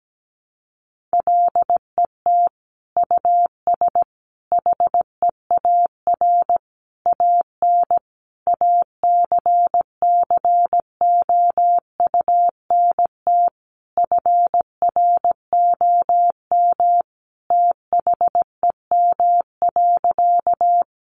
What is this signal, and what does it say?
Morse code, heard as a keyed tone, about 17 wpm, 700 Hz. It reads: LET US HEAR AN ACCOUNT FROM THEM.
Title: Sentences from Top 900 Words 17wpm